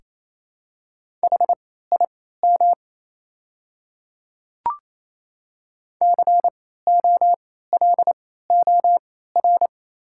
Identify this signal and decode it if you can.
Morse code, heard as a keyed tone, about 28 wpm, spaced as for 16 wpm, 700 Hz. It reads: HIM COLOR
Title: 3x Character Spacing - Top 300 Words - 28wpm